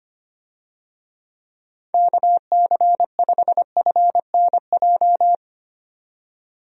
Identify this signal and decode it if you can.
Morse code, heard as a keyed tone, about 25 wpm, 700 Hz. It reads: KC5FNJ
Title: USA Call Signs - 6 Characters - 25wpm